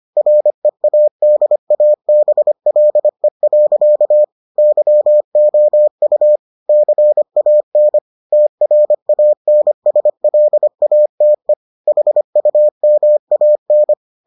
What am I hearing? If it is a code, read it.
Morse code, heard as a keyed tone, about 25 wpm, 600 Hz. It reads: READABLE. YOU CAN TRANSLATE HUMAN